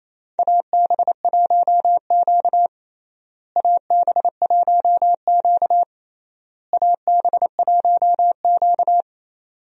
Morse code, heard as a keyed tone, about 27 wpm, 700 Hz. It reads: AB1Q AB1Q AB1Q
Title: USA Call Signs - 4 Characters - Repeated 3x - 28wpm